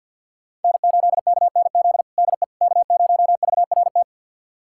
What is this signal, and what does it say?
Morse code, heard as a keyed tone, about 51 wpm, 700 Hz. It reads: N8KNZ DE K0VRT
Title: USA Callsign DE Callsign - 5 Characters - 50wpm